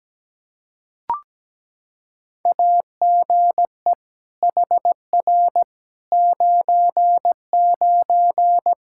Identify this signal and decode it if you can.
Morse code, heard as a keyed tone, about 17 wpm, 700 Hz. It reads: AGE HR 99
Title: Practice QSO - Exchange 3 - More Info - 17wpm